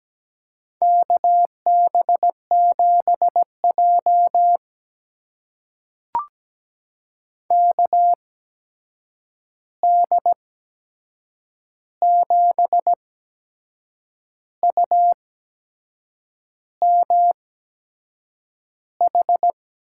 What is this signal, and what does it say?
Morse code, heard as a keyed tone, about 17 wpm, spaced as for 6 wpm, 700 Hz. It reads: KB7J KD7UMH
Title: USA Call Signs - 8x Character Spacing - 17wpm